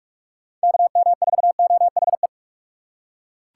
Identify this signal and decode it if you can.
Morse code, heard as a keyed tone, about 44 wpm, 700 Hz. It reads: KM4OHE